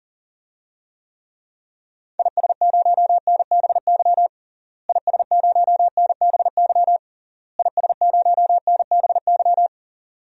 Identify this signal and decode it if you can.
Morse code, heard as a keyed tone, about 40 wpm, 700 Hz. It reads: IS0NBY IS0NBY IS0NBY
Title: DXCC-100 Call Signs - 6 Characters - Repeated 3x - 40wpm